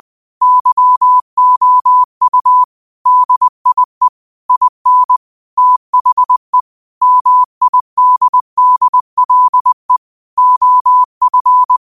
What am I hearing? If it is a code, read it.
Morse code, heard as a keyed tone, about 20 wpm, 1000 Hz. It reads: YOU DIE IN THE MIDDLE OF
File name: qotdfiles / 20wpm